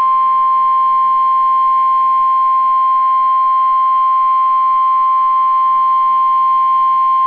Emergency Alert System (SAME) tones (the 1050 Hz attention tone). It is heard from 1.0 to 6.5 s.